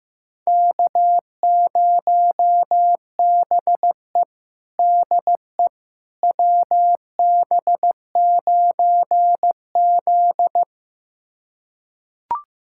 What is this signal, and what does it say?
Morse code, heard as a keyed tone, about 15 wpm, 700 Hz. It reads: K0BE DE WB9Z
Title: USA Callsign DE Callsign - 4 Characters - 15wpm